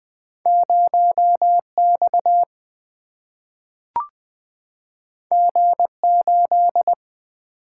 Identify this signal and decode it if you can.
Morse code, heard as a keyed tone, about 20 wpm, 700 Hz. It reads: 0 X G 8
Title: Single-Letter Single-Number 20wpm